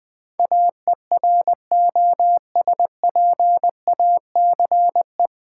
Morse code, heard as a keyed tone, about 20 wpm, 700 Hz. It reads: AEROSPACE
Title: Practice QSO - Exchange 3 - More Info - 20wpm